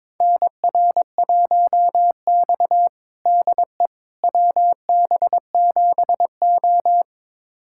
Morse code, heard as a keyed tone, about 22 wpm, 700 Hz. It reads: NR1X DE WB7O
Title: USA Callsign DE Callsign - 4 Characters - 22wpm